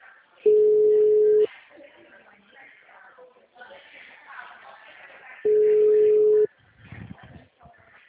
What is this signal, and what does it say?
Ringback tone (Europe, 425 Hz).